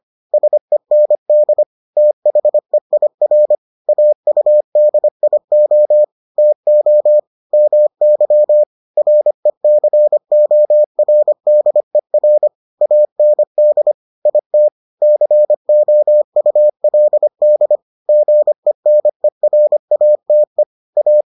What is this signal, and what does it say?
Morse code, heard as a keyed tone, about 25 wpm, 600 Hz. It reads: SEND THEIR AUDIO TO MY RECORDER AND IT COULD GENERATE A